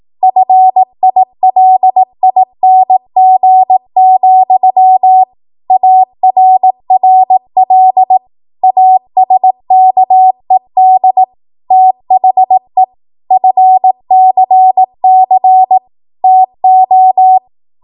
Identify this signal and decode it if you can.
Morse code, heard as a keyed tone, about 18 wpm, 750 Hz. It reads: FILING, ARRL ASKED THE FCC TO
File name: Morse / 18WPM